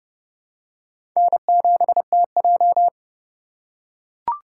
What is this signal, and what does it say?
Morse code, heard as a keyed tone, about 28 wpm, 700 Hz. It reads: N7TJ
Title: USA Call Signs - 4 Characters - 2x Character Spacing - 30wpm